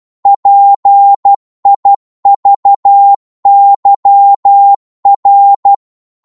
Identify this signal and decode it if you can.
Morse code, heard as a keyed tone, about 12 wpm, 800 Hz. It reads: PIVYR
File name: Morse / ltr-12wpm